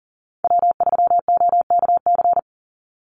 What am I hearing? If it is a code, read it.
Morse code, heard as a keyed tone, about 40 wpm, 700 Hz. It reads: W3OKC
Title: Call Signs Repeated 3x - 40wpm